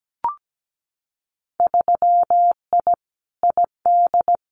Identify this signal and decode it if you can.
Morse code, heard as a keyed tone, about 17 wpm, 700 Hz. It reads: 3I ID
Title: Winter Field Day Exchanges 17wpm